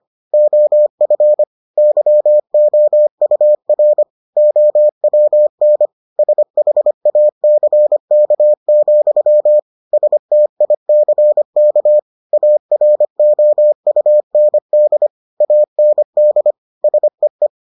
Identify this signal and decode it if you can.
Morse code, heard as a keyed tone, about 25 wpm, 600 Hz. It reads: OF YOUR OWN SHACK, STICK AROUND AND SEE